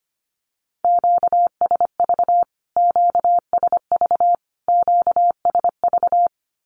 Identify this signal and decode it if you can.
Morse code, heard as a keyed tone, about 25 wpm, 700 Hz. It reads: QSV QSV QSV
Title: Q-Codes Repeated 3x 25wpm